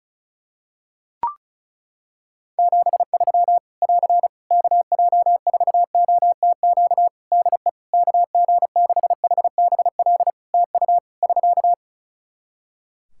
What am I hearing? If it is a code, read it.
Morse code, heard as a keyed tone, about 34 wpm, 700 Hz. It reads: 73 <AR> KJ4OTQ DE KG6HBL TU <SK>
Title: Practice QSO - Exchange 4 - Ending - 35wpm